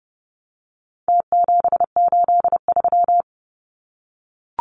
Morse code, heard as a keyed tone, about 30 wpm, 700 Hz. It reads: T783
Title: Numbers - Contest Style - 4-Digit 30wpm